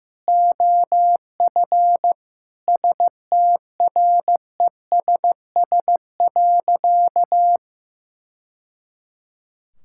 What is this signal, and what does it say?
Morse code, heard as a keyed tone, about 15 wpm, 700 Hz. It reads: OF STRESS.
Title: Sentences from Top 1000 Words - Review Part 6 - 15wpm